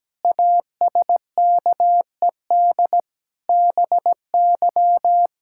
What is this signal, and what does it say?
Morse code, heard as a keyed tone, about 17 wpm, 700 Hz. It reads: ASKED BY
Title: Pangram Challenge v2 - 17wpm